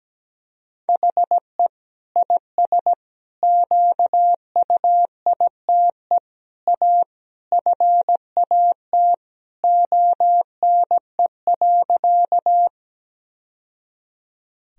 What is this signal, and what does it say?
Morse code, heard as a keyed tone, about 17 wpm, 700 Hz. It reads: HE IS QUITE A FAT ONE.
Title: Sentences from Top 1000 Words - Review Part 4 - 17wpm